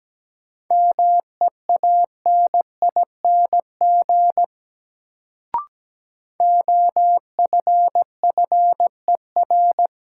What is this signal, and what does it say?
Morse code, heard as a keyed tone, about 17 wpm, 700 Hz. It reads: MEANING OFFER
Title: Sentences from Top 900 Words - New Word Drill - 17wpm